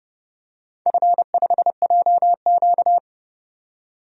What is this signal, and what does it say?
Morse code, heard as a keyed tone, about 30 wpm, 700 Hz. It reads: F5JQ
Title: DXCC-100 Call Signs - 4 Characters - 30wpm